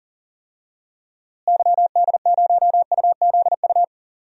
Morse code, heard as a keyed tone, about 40 wpm, 700 Hz. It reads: YD0UZU
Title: DXCC-100 Call Signs - 6 Characters - Repeated 3x - 40wpm